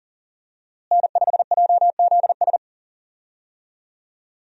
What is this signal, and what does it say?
Morse code, heard as a keyed tone, about 37 wpm, 700 Hz. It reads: N5JZS
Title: USA Call Signs - 5 Characters - Repeated 2x - 40wpm